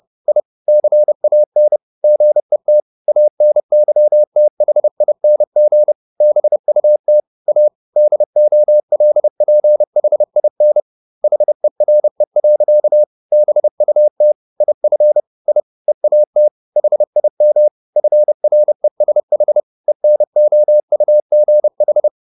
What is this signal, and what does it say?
Morse code, heard as a keyed tone, about 30 wpm, 600 Hz. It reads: I CAN GET ANYTHING BUT A DOLPHIN HERE. BUT IF I EAT HIM FRESH ENOUGH
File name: the old man and the sea / the old man and the sea 1 30wpm